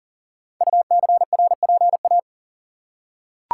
Morse code, heard as a keyed tone, about 39 wpm, 700 Hz. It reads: UCRPA
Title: Code Groups - 5 Letter:Numbers - 40wpm